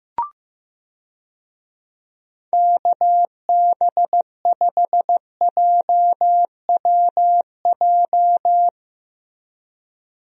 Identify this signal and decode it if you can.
Morse code, heard as a keyed tone, about 15 wpm, 700 Hz. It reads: KB5JWJ
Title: USA Call Signs - 6 Characters - Repeated 2x - 15wpm